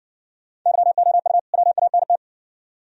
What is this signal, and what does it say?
Morse code, heard as a keyed tone, about 57 wpm, 700 Hz. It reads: YOU WANT